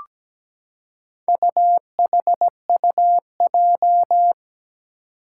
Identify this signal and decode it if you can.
Morse code, heard as a keyed tone, about 17 wpm, 700 Hz. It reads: UHUJ
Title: Four Challenging Letter-Number Combinations 17wpm